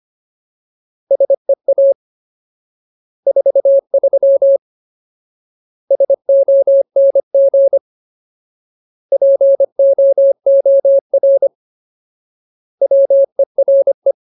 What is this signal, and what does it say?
Morse code, heard as a keyed tone, about 25 wpm, 550 Hz. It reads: SEA 43 SONG POOR WERE
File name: Ultimate Word List / 25WPM Ultimate